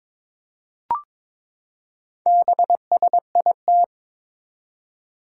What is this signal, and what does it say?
Morse code, heard as a keyed tone, about 22 wpm, 700 Hz. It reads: BSIT
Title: Four Challenging Letter-Number Combinations 22wpm